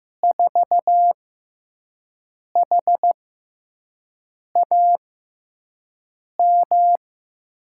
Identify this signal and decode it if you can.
Morse code, heard as a keyed tone, about 15 wpm, spaced as for 5 wpm, 700 Hz. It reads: 4HAM
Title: USA Call Signs - 6 Characters - 6x Character Spacing - 15wpm